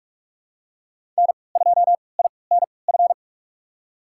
Morse code, heard as a keyed tone, about 46 wpm, spaced as for 25 wpm, 700 Hz. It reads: N2INF
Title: USA Call Signs - 5 Characters - 3x Character Spacing - 45wpm